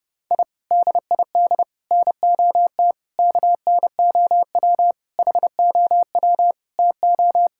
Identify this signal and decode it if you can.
Morse code, heard as a keyed tone, about 30 wpm, 700 Hz. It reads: I DID NOT KNOW HOW TO
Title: Sentences from Top 1000 Words - Review Part 6 - 30wpm